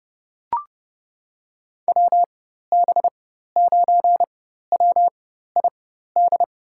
Morse code, heard as a keyed tone, about 31 wpm, spaced as for 14 wpm, 700 Hz. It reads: WB9WID
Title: USA Call Signs - 4x Character Spacing - 30wpm